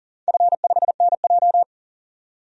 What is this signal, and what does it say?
Morse code, heard as a keyed tone, about 40 wpm, 700 Hz. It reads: F5NJ
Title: DXCC-100 Call Signs - 4 Characters - 40wpm